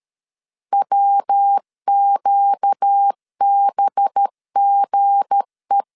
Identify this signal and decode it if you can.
Morse code, heard as a keyed tone, about 13 wpm, 800 Hz. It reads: WQBGE
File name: CW / DET / 1 9 1500C S C 12wpm 3811